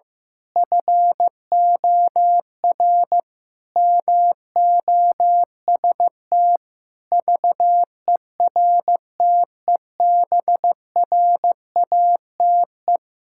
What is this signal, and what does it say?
Morse code, heard as a keyed tone, about 14 wpm, 700 Hz. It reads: FOR MOST VERTEBRATE
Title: Pangram Challenge v2 - 15wpm